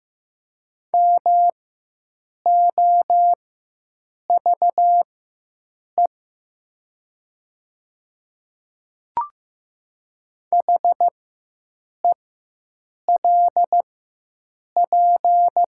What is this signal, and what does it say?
Morse code, heard as a keyed tone, about 15 wpm, spaced as for 7 wpm, 700 Hz. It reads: MOVE HELP
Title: 4x Character Spacing - Top 300 Words - 15wpm